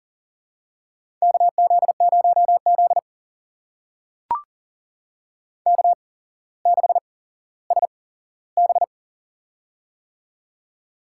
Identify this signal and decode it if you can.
Morse code, heard as a keyed tone, about 40 wpm, spaced as for 17 wpm, 690 Hz. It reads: KZ0Z K6SB
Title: USA Call Signs - 4 Characters - 8x Character Spacing - 40wpm